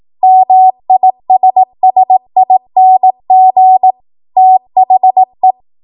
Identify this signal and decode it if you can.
Morse code, heard as a keyed tone, about 18 wpm, 750 Hz. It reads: MISSING THE
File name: Archive / 18 WPM / 18WPM